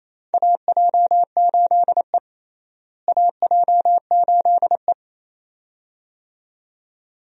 Morse code, heard as a keyed tone, about 28 wpm, 700 Hz. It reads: AJ8E AJ8E
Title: USA Call Signs - 4 Characters - Repeated 2x - 28wpm